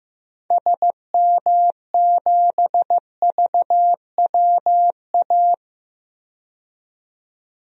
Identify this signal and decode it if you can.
Morse code, heard as a keyed tone, about 15 wpm, 700 Hz. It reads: SM7VWA
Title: DXCC-100 Call Signs - 6 Characters - 15wpm